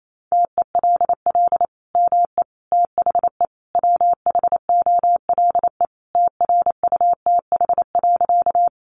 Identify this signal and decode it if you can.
Morse code, heard as a keyed tone, about 28 wpm, 700 Hz. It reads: TELL ME THE WHOLE TRUTH.
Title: Sentences from Top 800 Words 28wpm